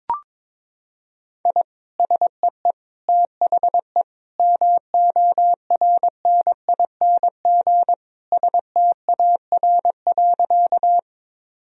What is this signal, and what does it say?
Morse code, heard as a keyed tone, about 22 wpm, 700 Hz. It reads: I SEE THE MORNING STAR.
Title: Sentences from Top 1000 Words - Review Part 2 - 22wpm